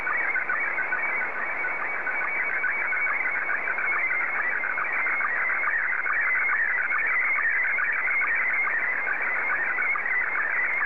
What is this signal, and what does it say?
SSTV (slow-scan television), Scottie 1.
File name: sstv / S1-JA3QJQ